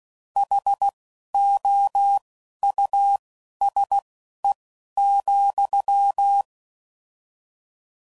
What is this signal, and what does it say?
Morse code, heard as a keyed tone, about 16 wpm, spaced as for 11 wpm, 800 Hz. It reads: HOUSE,